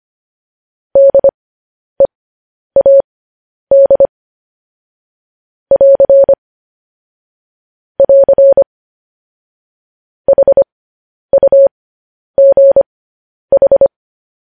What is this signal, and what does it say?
Morse code, heard as a keyed tone, about 25 wpm, spaced as for 10 wpm, 550 Hz. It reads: DEAD <AR> <AR> HUGH
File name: podcast / News headlines-1x-25x10